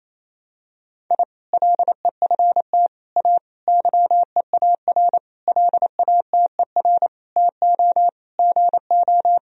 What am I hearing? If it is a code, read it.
Morse code, heard as a keyed tone, about 28 wpm, 700 Hz. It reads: I LEFT A YEAR LATER TO GO
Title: Sentences from Top 1000 Words - Review Part 4 - 28wpm